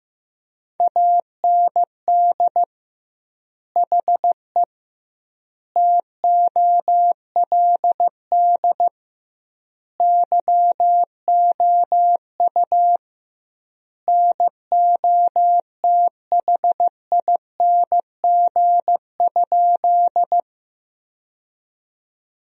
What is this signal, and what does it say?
Morse code, heard as a keyed tone, about 15 wpm, 700 Hz. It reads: AND HE TOLD YOU NOTHING?
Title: 2x Word-Spacing - Sentences from Top 500 Words - 15wpm